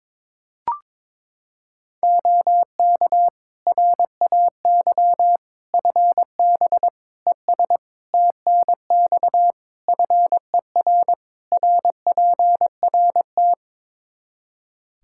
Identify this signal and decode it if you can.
Morse code, heard as a keyed tone, about 22 wpm, 700 Hz. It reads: OK RAY FB ES TNX FER RPRT
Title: Practice QSO - Exchange 2 - Info - 22wpm